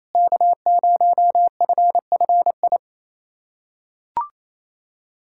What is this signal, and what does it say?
Morse code, heard as a keyed tone, about 28 wpm, 700 Hz. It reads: K0FFI
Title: USA Callsign DE Callsign 28wpm